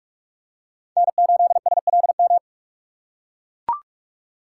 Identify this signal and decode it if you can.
Morse code, heard as a keyed tone, about 43 wpm, 700 Hz. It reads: N8SLM
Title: USA Call Signs - 3x Character Spacing - 45wpm